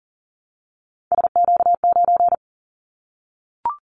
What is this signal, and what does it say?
Morse code, heard as a keyed tone, about 40 wpm, 700 Hz. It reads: SQ9
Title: Callsign Prefixes for Top 25 DXCC - Repeated 3x - 40wpm